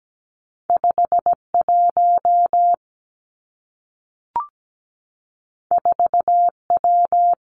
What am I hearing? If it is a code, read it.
Morse code, heard as a keyed tone, about 17 wpm, 700 Hz. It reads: 5 1 4 W
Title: Two Challenging Letter-Number Combinations 17wpm